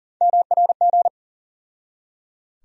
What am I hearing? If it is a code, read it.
Morse code, heard as a keyed tone, about 40 wpm, 700 Hz. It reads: MRG
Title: USA Call Sign Postfixes - 3 Letters 40wpm